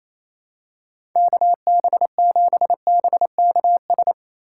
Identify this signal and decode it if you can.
Morse code, heard as a keyed tone, about 27 wpm, 700 Hz. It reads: KB7BKS
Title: USA Call Signs - 2x Character Spacing - 28wpm